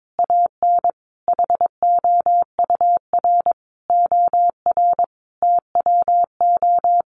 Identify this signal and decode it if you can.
Morse code, heard as a keyed tone, about 22 wpm, 700 Hz. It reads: AN HOUR OR TWO
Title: Sentences from Top 700 Words 22wpm